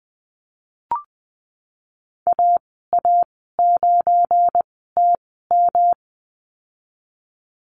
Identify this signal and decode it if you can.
Morse code, heard as a keyed tone, about 20 wpm, spaced as for 15 wpm, 700 Hz. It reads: AA9TM
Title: 173 Lesson 28g - Number 9 - Call Signs 20wpm